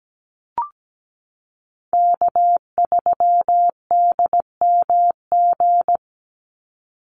Morse code, heard as a keyed tone, about 17 wpm, 700 Hz. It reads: K3DMG
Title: Call Signs - 17wpm